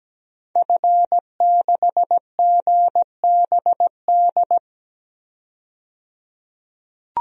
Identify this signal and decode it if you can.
Morse code, heard as a keyed tone, about 17 wpm, 700 Hz. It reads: F6GBD
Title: DXCC-100 Call Signs - 5 Characters - 17wpm